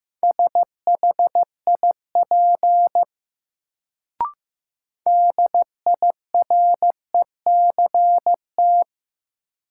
Morse code, heard as a keyed tone, about 15 wpm, 700 Hz. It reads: SHIP DIRECT
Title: Sentences from Top 400 Words - New Word Drill - 15wpm